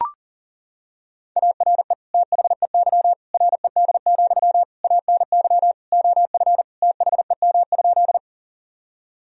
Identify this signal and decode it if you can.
Morse code, heard as a keyed tone, about 37 wpm, 700 Hz. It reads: ARE THEY RED, ANY OF THEM?
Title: Sentences from Top 1000 Words - Review Part 2 - 40wpm